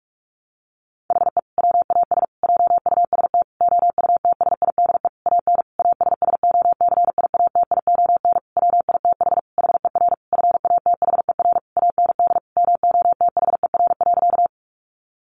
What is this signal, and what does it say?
Morse code, heard as a keyed tone, about 44 wpm, 700 Hz. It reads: HE WAS JUST OUTSIDE AN ASSOCIATION WITH HER FATHER AND MOTHER.
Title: Sentences from Top 1100 Words 45wpm